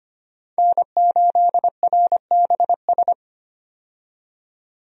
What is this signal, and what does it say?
Morse code, heard as a keyed tone, about 25 wpm, 700 Hz. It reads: N8RBS
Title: USA Call Signs - 5 Characters - 25wpm